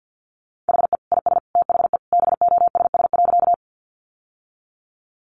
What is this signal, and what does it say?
Morse code, heard as a keyed tone, about 50 wpm, 700 Hz. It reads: HE IS THE BOSS.
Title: Sentences from Top 700 Words 50wpm